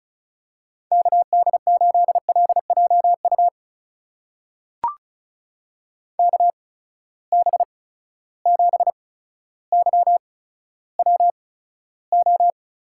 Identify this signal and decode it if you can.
Morse code, heard as a keyed tone, about 36 wpm, spaced as for 15 wpm, 700 Hz. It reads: KD8LJU KB7YWO
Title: USA Call Signs - 8x Character Spacing - 35wpm